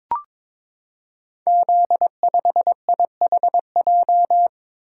Morse code, heard as a keyed tone, about 22 wpm, 700 Hz. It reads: Z5IHJ
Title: Code Groups - 5 Letter:Numbers - 22wpm